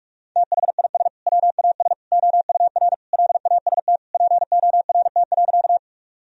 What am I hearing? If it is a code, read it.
Morse code, heard as a keyed tone, about 46 wpm, 700 Hz. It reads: THIS WAS OUR LAST PORT.
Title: Sentences from Top 1000 Words - Review Part 1 - 45wpm